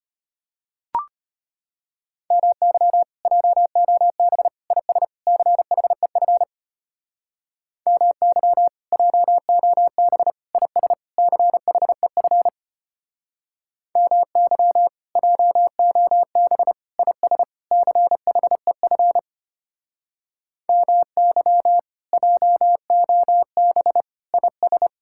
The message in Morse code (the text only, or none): MY JOB IS CHEF MY JOB IS CHEF MY JOB IS CHEF MY JOB IS